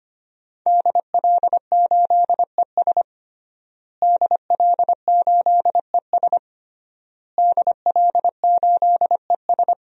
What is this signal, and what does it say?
Morse code, heard as a keyed tone, about 25 wpm, 700 Hz. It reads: DL8ES DL8ES DL8ES
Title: DXCC-100 Call Signs - 5 Characters - Repeated 3x - 25wpm